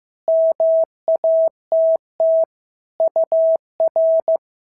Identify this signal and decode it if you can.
Morse code, heard as a keyed tone, about 15 wpm, 650 Hz. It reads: MATT UR
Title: SOTA QSO Practice 15wpm